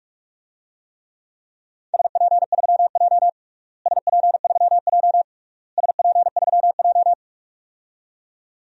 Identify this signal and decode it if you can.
Morse code, heard as a keyed tone, about 46 wpm, 700 Hz. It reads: SP3J SP3J SP3J
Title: DXCC-100 Call Signs - 4 Characters - Repeated 3x - 45wpm